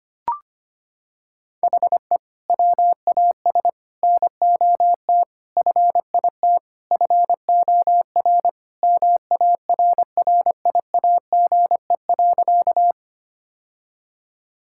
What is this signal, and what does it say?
Morse code, heard as a keyed tone, about 25 wpm, 700 Hz. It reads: HE WAS NOT FIT FOR MARRIAGE.
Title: Sentences from Top 1000 Words - Review Part 5 - 25wpm